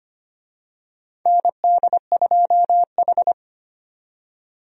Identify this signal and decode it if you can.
Morse code, heard as a keyed tone, about 25 wpm, 700 Hz. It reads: ND2H
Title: USA Call Signs - 4 Characters - 25wpm